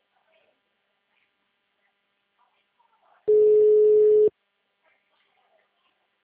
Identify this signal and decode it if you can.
Ringback tone (Europe, 425 Hz).